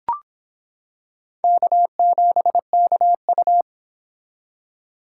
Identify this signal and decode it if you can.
Morse code, heard as a keyed tone, about 26 wpm, 700 Hz. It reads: K7KU